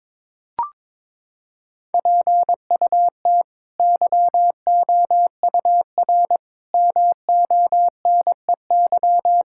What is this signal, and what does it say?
Morse code, heard as a keyed tone, about 22 wpm, 700 Hz. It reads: PUT YOUR MONEY
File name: practice / 22wpm / Sets of 8 Words - 22wpm